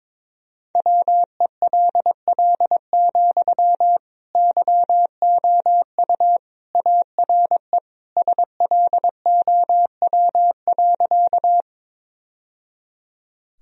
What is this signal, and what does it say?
Morse code, heard as a keyed tone, about 22 wpm, 700 Hz. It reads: WELL, YOU ARE SLOW.
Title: Sentences from Top 1000 Words - Review Part 2 - 22wpm